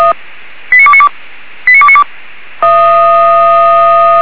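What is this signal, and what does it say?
Five-tone selective call (ZVEI), heard as quick runs of five tones, tones of about 70 ms. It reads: 89292 89292
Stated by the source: ZVEI five-tone selective call received by SDR.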